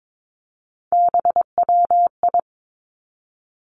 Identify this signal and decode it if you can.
Morse code, heard as a keyed tone, about 22 wpm, 700 Hz. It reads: BWI
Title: Three Challenging Letter-Number Combinations 22wpm